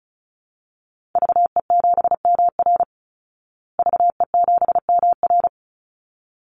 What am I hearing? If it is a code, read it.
Morse code, heard as a keyed tone, about 34 wpm, 700 Hz. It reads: VE7MR VE7MR
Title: Call Signs Repeated 3x - 35wpm